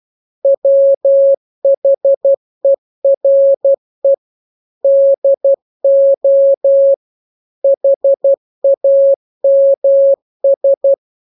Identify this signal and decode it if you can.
Morse code, heard as a keyed tone, about 12 wpm, 550 Hz. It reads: WHERE DO HAMS